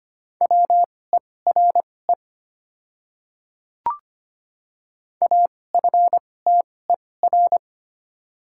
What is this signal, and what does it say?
Morse code, heard as a keyed tone, about 25 wpm, spaced as for 18 wpm, 700 Hz. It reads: WERE AFTER
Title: 2x Character Spacing - Top 200 Words - 25wpm